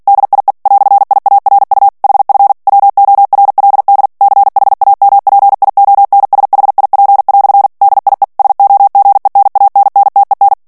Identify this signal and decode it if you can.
Morse code, heard as a keyed tone, about 48 wpm, 800 Hz. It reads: DIE "IARU HF WORLD CHAMPIONSHIP" DIE SOGENANNTEN
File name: morse mp3 / ALLEMAND texte clair 2 vitesse 45 mots minute